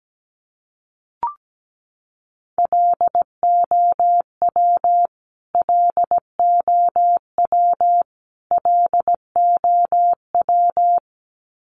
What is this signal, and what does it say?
Morse code, heard as a keyed tone, about 17 wpm, 700 Hz. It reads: LOW LOW LOW
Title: Top 500 Words Repeated 17wpm